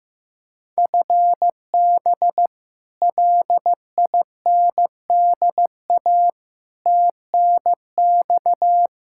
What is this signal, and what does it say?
Morse code, heard as a keyed tone, about 15 wpm, 700 Hz. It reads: FB LINDA TNX
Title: Practice QSO - Exchange 3 - More Info - 15wpm